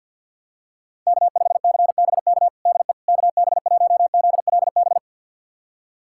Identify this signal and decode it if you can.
Morse code, heard as a keyed tone, about 46 wpm, 700 Hz. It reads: K5CBK DE KB1ZLB
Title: USA Callsign DE Callsign 50wpm